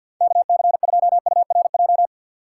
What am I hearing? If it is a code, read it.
Morse code, heard as a keyed tone, about 53 wpm, 700 Hz. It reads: KC2URJ
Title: USA Call Signs - 6x Character Spacing - 50wpm